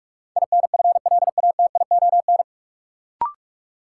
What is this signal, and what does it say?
Morse code, heard as a keyed tone, about 46 wpm, 700 Hz. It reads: INFLATION
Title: Sentences from Top 1100 Words - New Word Drill - 45wpm